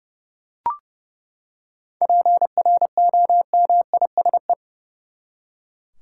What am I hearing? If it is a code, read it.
Morse code, heard as a keyed tone, about 30 wpm, 700 Hz. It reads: PROMISE
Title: Seven Letter Words - 30wpm